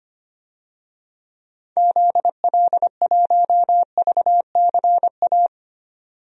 Morse code, heard as a keyed tone, about 25 wpm, 700 Hz. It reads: ZL1VCA
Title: DXCC-100 Call Signs - 6 Characters - 25wpm